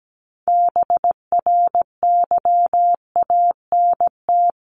Morse code, heard as a keyed tone, about 17 wpm, 700 Hz. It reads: BRYANT
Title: CWT Exchange 17wpm